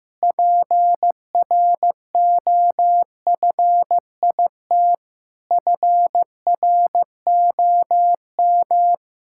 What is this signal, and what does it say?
Morse code, heard as a keyed tone, about 16 wpm, 700 Hz. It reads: PROFIT FROM
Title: Sentences from Top 1000 Words - Review Part 6 - 15wpm